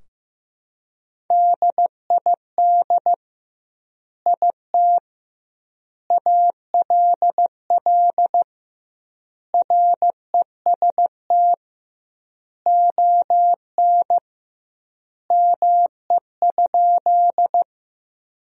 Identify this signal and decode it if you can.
Morse code, heard as a keyed tone, about 15 wpm, 700 Hz. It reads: DID IT ALL REST ON ME?
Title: 2x Word-Spacing - Sentences from Top 500 Words - 15wpm